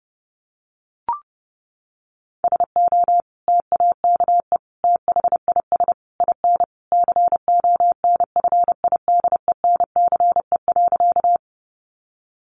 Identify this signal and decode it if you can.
Morse code, heard as a keyed tone, about 28 wpm, 700 Hz. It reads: SO TAKE THIS IN CONFIDENCE.
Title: Sentences from Top 1500 Words 30wpm